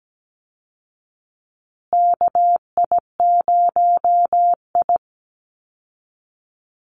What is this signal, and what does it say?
Morse code, heard as a keyed tone, about 17 wpm, 700 Hz. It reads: KI0I
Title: Call Signs Repeated 2x - 17wpm